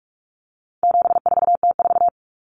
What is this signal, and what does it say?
Morse code, heard as a keyed tone, about 44 wpm, 700 Hz. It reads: 74T4
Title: Common Cut Numbers - 4-Digit 45wpm